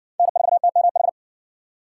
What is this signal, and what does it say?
Morse code, heard as a keyed tone, about 60 wpm, 700 Hz. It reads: N4TMH